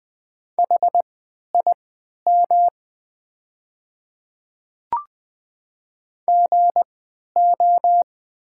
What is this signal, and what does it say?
Morse code, heard as a keyed tone, about 20 wpm, spaced as for 11 wpm, 700 Hz. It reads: HIM GO
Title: 3x Character Spacing - Top 500 Words - 20wpm